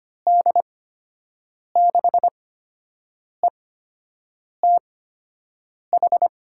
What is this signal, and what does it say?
Morse code, heard as a keyed tone, about 25 wpm, spaced as for 7 wpm, 700 Hz. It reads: D6ETH
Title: USA Call Signs - 6 Characters - 8x Character Spacing - 25wpm